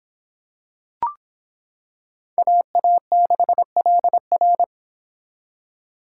Morse code, heard as a keyed tone, about 26 wpm, 700 Hz. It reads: AA6LR